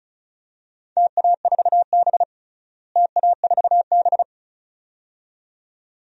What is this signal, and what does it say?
Morse code, heard as a keyed tone, about 35 wpm, 700 Hz. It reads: TA4B TA4B
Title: DXCC-100 Call Signs - 4 Characters - Repeated 2x - 35wpm